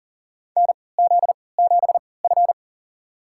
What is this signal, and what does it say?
Morse code, heard as a keyed tone, about 40 wpm, spaced as for 23 wpm, 700 Hz. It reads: NZ7F
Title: USA Call Signs - 3x Character Spacing - 40wpm